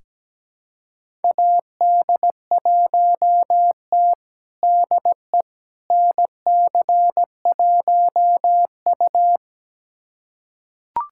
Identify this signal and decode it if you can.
Morse code, heard as a keyed tone, about 17 wpm, 700 Hz. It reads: AD1T DE NC1U
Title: USA Callsign DE Callsign - 4 Characters - 17wpm